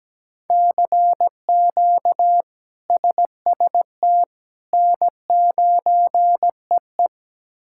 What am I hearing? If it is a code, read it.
Morse code, heard as a keyed tone, about 17 wpm, 700 Hz. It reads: CQ SST N9EE
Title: K1USN SST QSO Practice - 17wpm